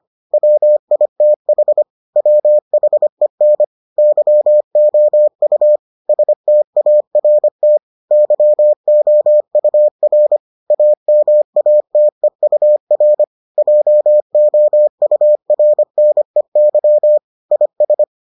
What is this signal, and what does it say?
Morse code, heard as a keyed tone, about 25 wpm, 600 Hz. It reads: WITH WHEN YOU START YOUR AMATEUR JOURNEY IS